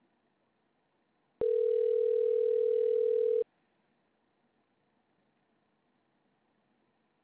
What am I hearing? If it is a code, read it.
Ringback tone (North America), 440 + 480 Hz.